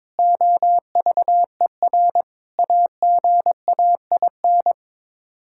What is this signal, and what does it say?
Morse code, heard as a keyed tone, about 22 wpm, 700 Hz. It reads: OVER AGAIN
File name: practice / 22wpm / Sets of 7 Words - 22wpm